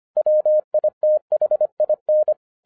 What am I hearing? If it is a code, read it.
Morse code, heard as a keyed tone, about 25 wpm, 610 Hz. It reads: WITHIN
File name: 2012-01-mp3 / n14-FUN WITH NUMBERS-25wpm